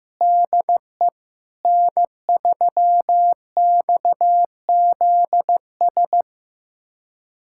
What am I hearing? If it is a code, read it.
Morse code, heard as a keyed tone, about 15 wpm, 700 Hz. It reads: DE N3XZS
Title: USA Callsign DE Callsign 15wpm